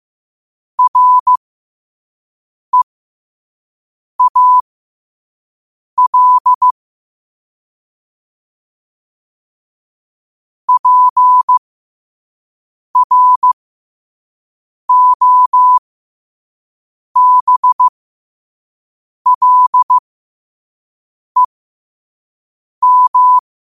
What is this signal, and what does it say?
Morse code, heard as a keyed tone, about 15 wpm, spaced as for 5 wpm, 1000 Hz. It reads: REAL PROBLEM
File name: qotdfiles / 5wpm